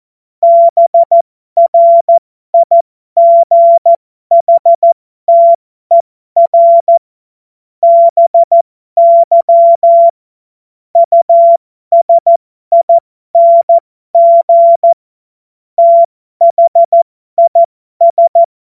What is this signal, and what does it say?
Morse code, heard as a keyed tone, about 14 wpm, spaced as for 12 wpm, 680 Hz. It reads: BRIGHTER BY USING THIS
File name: Morsekurs / Witze